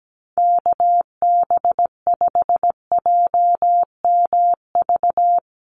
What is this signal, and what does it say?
Morse code, heard as a keyed tone, about 17 wpm, 700 Hz. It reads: KB5JMV
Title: USA Call Signs 17wpm